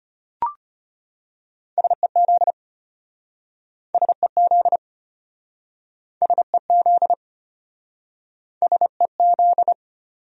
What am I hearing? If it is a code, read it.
Morse code, heard as a keyed tone, about 33 wpm, 700 Hz. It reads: SEZ SEZ SEZ SEZ